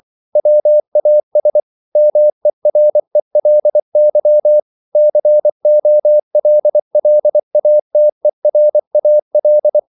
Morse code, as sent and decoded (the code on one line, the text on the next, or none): .-- .- ... / -- . .-. .-.. -.-- / -.-. --- .-.. .-.. .- - . .-. .- .-..
WAS MERELY COLLATERAL